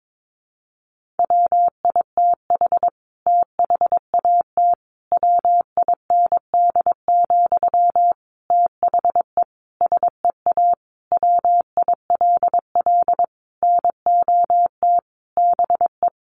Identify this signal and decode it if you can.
Morse code, heard as a keyed tone, about 22 wpm, 700 Hz. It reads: WITH THAT WIND, THE SEA WILL NOT BE
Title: Sentences from Top 1400 Words 22wpm